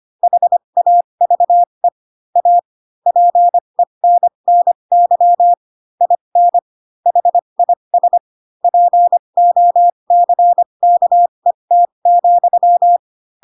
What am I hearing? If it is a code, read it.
Morse code, heard as a keyed tone, about 25 wpm, spaced as for 22 wpm, 700 Hz. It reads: HAVE A PENNY IN HIS POCKET,